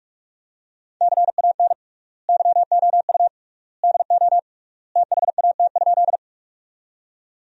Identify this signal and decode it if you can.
Morse code, heard as a keyed tone, about 46 wpm, 700 Hz. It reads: CAN YOU DO THAT?
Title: 3x Word-Spacing - Sentences from Top 100 Words - 45wpm